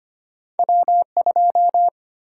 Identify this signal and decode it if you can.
Morse code, heard as a keyed tone, about 25 wpm, 700 Hz. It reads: W2
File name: Two Letter-Number Combinations - 25wpm